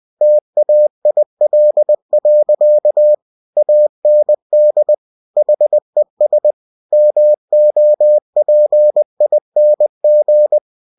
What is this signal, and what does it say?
Morse code, heard as a keyed tone, about 20 wpm, 600 Hz. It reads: TAIL. AND HES MOPING
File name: cw-books / winnie-the-pooh / winnie-20wpm-page-0029